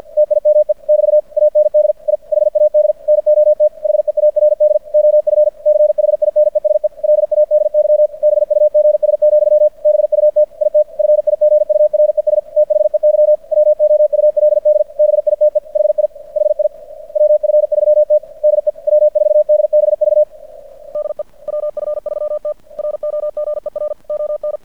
Morse code, heard as a keyed tone, about 50 wpm, 600 Hz. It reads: TIME <BT> AND I HAD NOT HEARD OF CHINESE LADY BUGS, BUT IT FIGURES THEY WOULD BITE HI HI WF3T DE W4DBV DE WF3T ROGER ON
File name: archived / CRSnet